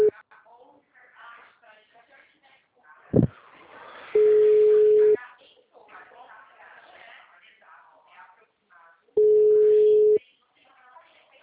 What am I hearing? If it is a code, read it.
Ringback tone (Europe, 425 Hz), with speech.